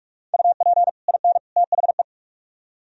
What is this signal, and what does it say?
Morse code, heard as a keyed tone, about 44 wpm, 700 Hz. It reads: UP IN THE